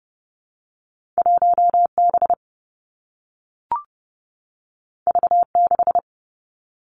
Morse code, heard as a keyed tone, about 30 wpm, 700 Hz. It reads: 1 B V 6
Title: Two Challenging Letter-Number Combinations 30wpm